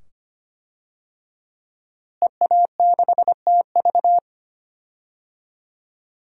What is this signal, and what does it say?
Morse code, heard as a keyed tone, about 25 wpm, 700 Hz. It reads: EA6TV
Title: DXCC-100 Call Signs - 5 Characters - Repeated 2x - 25wpm